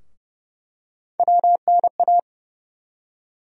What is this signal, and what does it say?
Morse code, heard as a keyed tone, about 30 wpm, 700 Hz. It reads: WNA
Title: USA Call Sign Postfixes - 3 Letters 30wpm